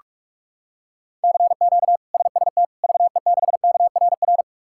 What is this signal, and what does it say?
Morse code, heard as a keyed tone, about 41 wpm, 700 Hz. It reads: CQ SST VE6KRR
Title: K1USN SST QSO Practice - 45wpm